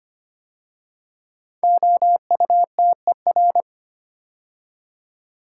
Morse code, heard as a keyed tone, about 25 wpm, 700 Hz. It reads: OUTER
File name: 5 Letter Words - First 3 Letters Given - 25wpm